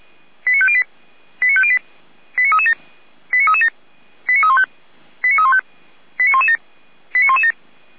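Five-tone selective call (ZVEI), heard as quick runs of five tones, tones of about 70 ms. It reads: 89508 89508 89337 89337 89315 89315 89117 89117
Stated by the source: ZVEI five-tone selective call received with an SDR.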